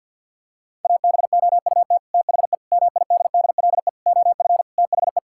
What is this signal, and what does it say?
Morse code, heard as a keyed tone, about 50 wpm, 700 Hz. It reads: ABOUT THE MIDDLE OF THE
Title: Sentences from Top 1000 Words - Review Part 6 - 50wpm